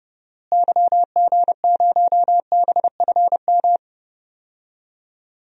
Morse code, heard as a keyed tone, about 30 wpm, 700 Hz. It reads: YG0BFM
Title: DXCC-100 Call Signs - 6 Characters - Repeated 2x - 30wpm